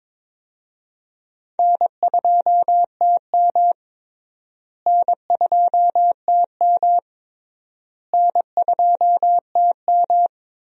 Morse code, heard as a keyed tone, about 22 wpm, 700 Hz. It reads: N2TM N2TM N2TM
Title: USA Call Signs - 4 Characters - Repeated 3x - 22wpm